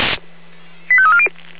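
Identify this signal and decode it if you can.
Five-tone selective call (ZVEI), heard as quick runs of five tones, tones of about 70 ms. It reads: 85340
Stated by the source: ZVEI five-tone selective call 85320 received by SDR.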